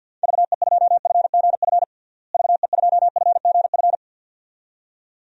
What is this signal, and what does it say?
Morse code, heard as a keyed tone, about 52 wpm, 700 Hz. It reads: VE2FGF VE2FGF
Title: DXCC-100 Call Signs - 6 Characters - Repeated 2x - 50wpm